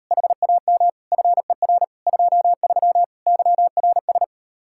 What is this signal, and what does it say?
Morse code, heard as a keyed tone, about 38 wpm, 700 Hz. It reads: HAM FER 23 YRS